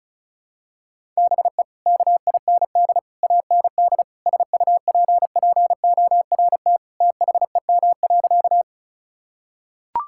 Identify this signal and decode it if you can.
Morse code, heard as a keyed tone, about 34 wpm, 700 Hz. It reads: BE KIND AND SUPPORT THEM.
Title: Sentences from Top 1000 Words - Review Part 5 - 35wpm